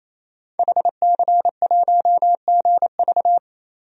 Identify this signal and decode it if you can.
Morse code, heard as a keyed tone, about 28 wpm, 700 Hz. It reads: HC1GV
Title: DXCC-100 Call Signs - 5 Characters - 28wpm